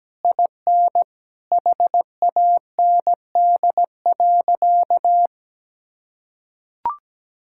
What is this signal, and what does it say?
Morse code, heard as a keyed tone, about 17 wpm, 700 Hz. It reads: IN HAND.
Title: Sentences from Top 1000 Words - Review Part 5 - 17wpm